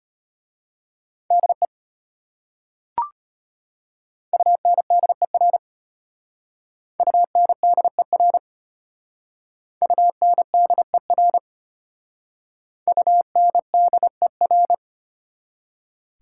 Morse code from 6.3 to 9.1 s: UNDER

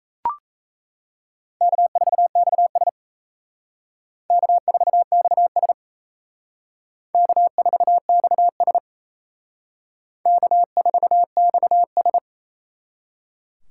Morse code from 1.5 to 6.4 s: K4XS K4XS

